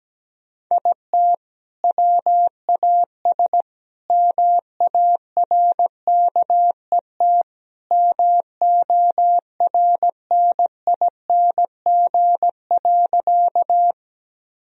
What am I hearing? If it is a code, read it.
Morse code, heard as a keyed tone, about 17 wpm, 700 Hz. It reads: IT WAS MARKET MORNING.
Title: Sentences from Top 1000 Words - Review Part 3 - 17wpm